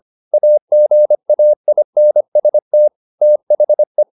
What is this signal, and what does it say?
Morse code, heard as a keyed tone, about 25 wpm, 600 Hz. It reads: AGAINST THE
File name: the old man and the sea part 1 25wpm